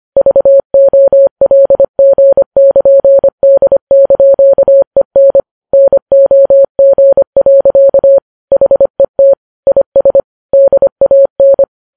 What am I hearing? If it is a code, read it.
Morse code, heard as a keyed tone, about 25 wpm, 550 Hz. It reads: VOLG<KN>D)EN NOG. HET IS DAN
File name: Files Web tls / Morse on9fv / morse25kps